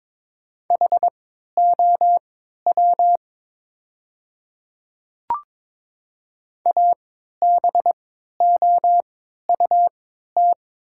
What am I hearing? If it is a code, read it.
Morse code, heard as a keyed tone, about 22 wpm, spaced as for 13 wpm, 700 Hz. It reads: HOW ABOUT
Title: 3x Character Spacing - Top 100 Words - 22wpm